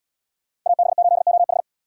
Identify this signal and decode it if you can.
Morse code, heard as a keyed tone, about 73 wpm, 700 Hz. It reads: AB8OH